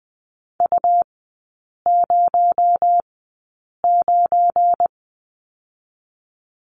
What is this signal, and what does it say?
Morse code, heard as a keyed tone, about 20 wpm, spaced as for 8 wpm, 700 Hz. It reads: U09
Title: 218 Lesson 35e - Number 0 - Three Characters 20wpm